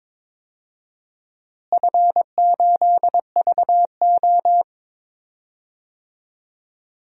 Morse code, heard as a keyed tone, about 22 wpm, 700 Hz. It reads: F8VO
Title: DXCC-100 Call Signs - 4 Characters - Repeated 3x - 22wpm